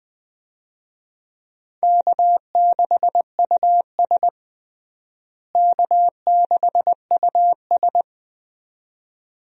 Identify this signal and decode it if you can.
Morse code, heard as a keyed tone, about 20 wpm, 700 Hz. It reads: K6US K6US
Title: USA Call Signs - 4 Characters - Repeated 2x - 20wpm